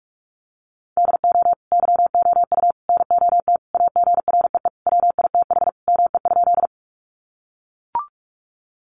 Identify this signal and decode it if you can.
Morse code, heard as a keyed tone, about 44 wpm, 700 Hz. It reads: DO YOU NOT AGREE WITH ME?
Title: Sentences from Top 1100 Words 45wpm